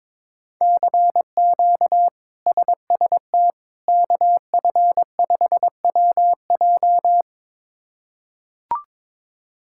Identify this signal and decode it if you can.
Morse code, heard as a keyed tone, about 22 wpm, 700 Hz. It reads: CQ SST KF5WJ
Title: K1USN SST QSO Practice - 22wpm